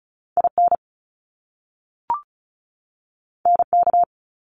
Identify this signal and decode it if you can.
Morse code, heard as a keyed tone, about 36 wpm, 700 Hz. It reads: IN NK
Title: Top Two Letter Combinations in Words 35wpm